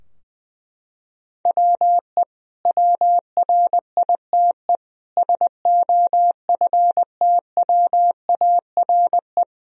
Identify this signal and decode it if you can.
Morse code, heard as a keyed tone, about 20 wpm, 700 Hz. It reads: WE WRITE SOFTWARE
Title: Sentences from Top 1000 Words - Review Part 3 - 20wpm